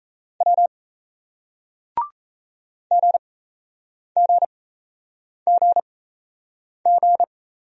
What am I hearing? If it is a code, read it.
Morse code, heard as a keyed tone, about 37 wpm, spaced as for 8 wpm, 700 Hz. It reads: W GGGG